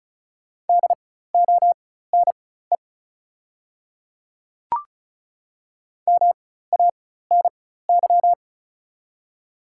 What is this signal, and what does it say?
Morse code, heard as a keyed tone, about 36 wpm, spaced as for 16 wpm, 690 Hz. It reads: DONE MANY
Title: 4x Character Spacing - Top 500 Words - 35wpm